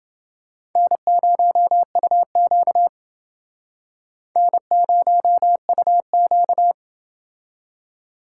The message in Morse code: N0UQ N0UQ

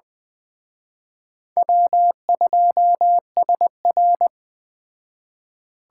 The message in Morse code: W2SR

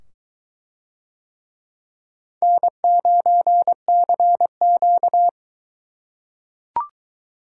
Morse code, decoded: N9CQ

